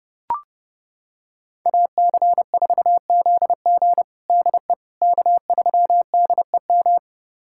Morse code, heard as a keyed tone, about 31 wpm, 700 Hz. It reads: AC4ZG DE K3DEM
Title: USA Callsign DE Callsign - 5 Characters - 30wpm